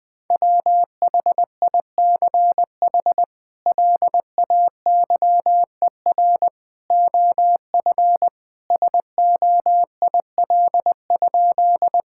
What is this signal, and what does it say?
Morse code, heard as a keyed tone, about 21 wpm, 700 Hz. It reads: WHICH LAYER OF SOIL?
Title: Sentences from Top 1600 Words 20wpm